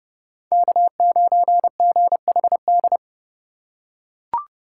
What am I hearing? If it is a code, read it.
Morse code, heard as a keyed tone, about 32 wpm, 700 Hz. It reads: K9GHD
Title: USA Call Signs - 2x Character Spacing - 30wpm